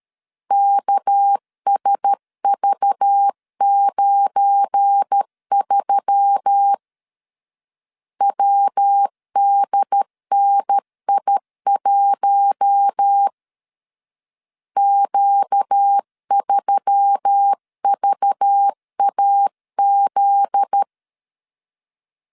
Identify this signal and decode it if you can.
Morse code, heard as a keyed tone, about 13 wpm, 800 Hz. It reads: KSV93 WDNI1 Q3VAZ
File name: LEZIONI / 7 GF9Z / 7 GF9Z 12wpm 2147